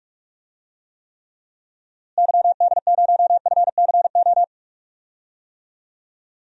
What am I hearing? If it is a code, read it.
Morse code, heard as a keyed tone, about 46 wpm, 700 Hz. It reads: YD0FCO
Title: DXCC-100 Call Signs - 6 Characters - Repeated 2x - 45wpm